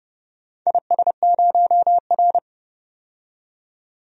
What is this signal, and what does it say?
Morse code, heard as a keyed tone, about 30 wpm, 700 Hz. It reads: IS0R
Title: DXCC-100 Call Signs - 4 Characters - 30wpm